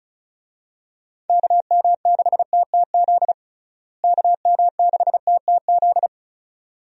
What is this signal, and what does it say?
Morse code, heard as a keyed tone, about 36 wpm, 700 Hz. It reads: KM6TTZ KM6TTZ
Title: USA Call Signs - 6 Characters - Repeated 2x - 35wpm